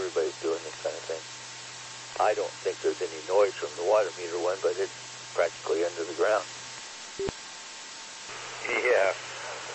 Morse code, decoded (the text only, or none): none